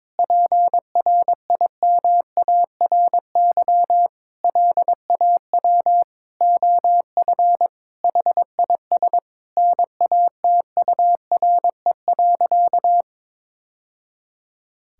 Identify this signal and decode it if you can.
Morse code, heard as a keyed tone, about 21 wpm, 700 Hz. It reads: PRIMARY LAW OF HIS NATURE.
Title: Sentences from Top 1000 Words - Review Part 5 - 22wpm